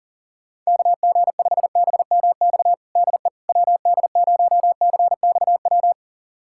Morse code, heard as a keyed tone, about 40 wpm, 700 Hz. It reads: KG5BMX DE WD0CXW
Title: USA Callsign DE Callsign - 6 Characters - 40wpm